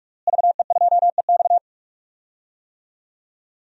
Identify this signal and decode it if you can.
Morse code, heard as a keyed tone, about 47 wpm, 700 Hz. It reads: VE2EX
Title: DXCC-100 Call Signs - 5 Characters - Repeated 2x - 45wpm